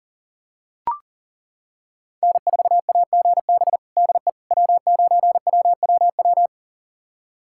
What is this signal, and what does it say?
Morse code, heard as a keyed tone, about 40 wpm, 700 Hz. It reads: N4AGB DE W9WWW
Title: USA Callsign DE Callsign - 5 Characters - 40wpm